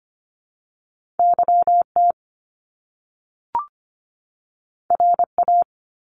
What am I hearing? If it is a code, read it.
Morse code, heard as a keyed tone, about 25 wpm, 700 Hz. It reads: YT RA
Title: Top Two Letter Combinations in Words 25wpm